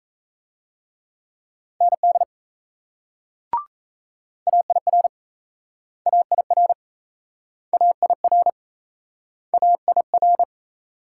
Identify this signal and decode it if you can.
Morse code, heard as a keyed tone, about 38 wpm, 700 Hz. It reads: ND AIR AIR AIR AIR